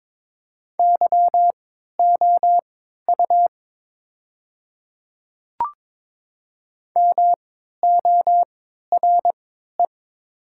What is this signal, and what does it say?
Morse code, heard as a keyed tone, about 22 wpm, spaced as for 13 wpm, 700 Hz. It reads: YOU MORE
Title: 3x Character Spacing - Top 200 Words - 22wpm